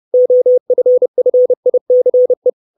Morse code, heard as a keyed tone, about 30 wpm, 500 Hz. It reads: OFFICE
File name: 632 Words 30WPM